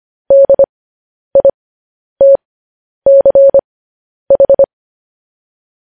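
Morse code, heard as a keyed tone, about 25 wpm, spaced as for 10 wpm, 550 Hz. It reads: DITCH
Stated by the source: Morse code news headlines.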